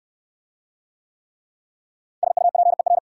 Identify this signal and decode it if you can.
Morse code, heard as a keyed tone, about 60 wpm, 690 Hz. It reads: SUPER